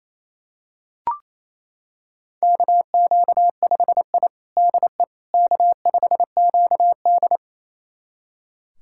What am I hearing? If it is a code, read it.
Morse code, heard as a keyed tone, about 28 wpm, 700 Hz. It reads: KQ5I DE K5QD
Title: USA Callsign DE Callsign - 4 Characters - 28wpm